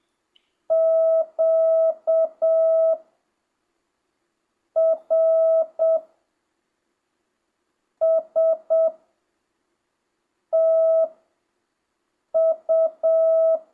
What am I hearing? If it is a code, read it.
Morse code, heard as a keyed tone, about 7 wpm, spaced as for 4 wpm, 650 Hz. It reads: QRSTU
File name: jh8cbh / 2025cw